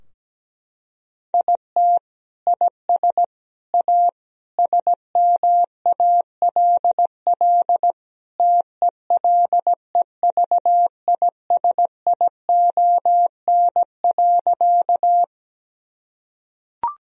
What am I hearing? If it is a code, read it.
Morse code, heard as a keyed tone, about 17 wpm, 700 Hz. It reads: IT IS A SMALL TELEVISION.
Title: Sentences from Top 1000 Words - Review Part 4 - 17wpm